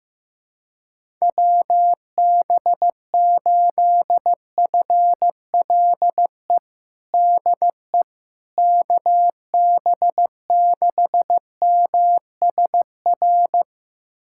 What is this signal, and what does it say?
Morse code, heard as a keyed tone, about 15 wpm, 700 Hz. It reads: WB8FLE DE KB6MSR
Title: USA Callsign DE Callsign - 6 Characters - 15wpm